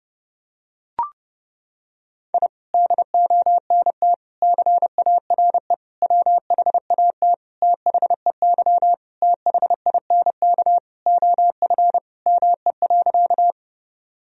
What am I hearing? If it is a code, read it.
Morse code, heard as a keyed tone, about 32 wpm, 700 Hz. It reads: I DONT CARE WHAT THEY THINK OF ME.
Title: Sentences from Top 1000 Words - Review Part 1 - 30wpm